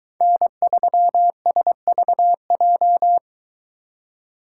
Morse code, heard as a keyed tone, about 22 wpm, 700 Hz. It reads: N3SVJ